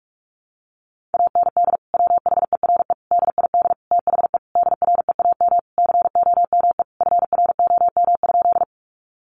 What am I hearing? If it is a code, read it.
Morse code, heard as a keyed tone, about 44 wpm, 700 Hz. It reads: AND WHERE DID THE DREAM COME FROM?
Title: Sentences from Top 1500 Words 45wpm